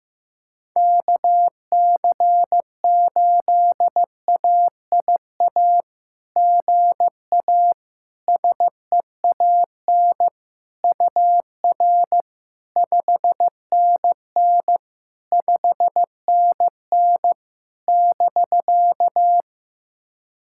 Morse code, heard as a keyed tone, about 15 wpm, 690 Hz. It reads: KC8AIA GA SEAN UR 5NN 5NN <BK>
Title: SOTA QSO Practice 15wpm